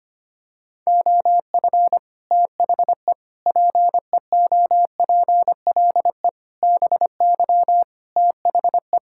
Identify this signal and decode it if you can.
Morse code, heard as a keyed tone, about 25 wpm, 700 Hz. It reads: OF THE PEOPLE BY THE
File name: Sets of 9 Words - 25wpm